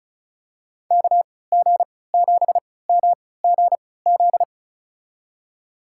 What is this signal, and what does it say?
Morse code, heard as a keyed tone, about 36 wpm, spaced as for 20 wpm, 700 Hz. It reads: KG7MGZ
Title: USA Call Signs - 6 Characters - 3x Character Spacing - 35wpm